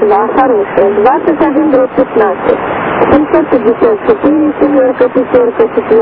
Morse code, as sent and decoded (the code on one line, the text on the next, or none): none